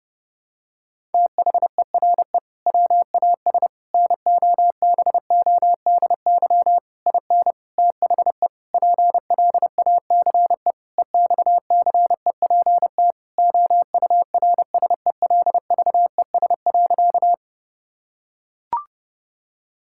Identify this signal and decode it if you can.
Morse code, heard as a keyed tone, about 32 wpm, 700 Hz. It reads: THERE WAS NOBODY IN THE PLACE EXCEPT OURSELVES.
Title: Sentences from Top 1600 Words 30wpm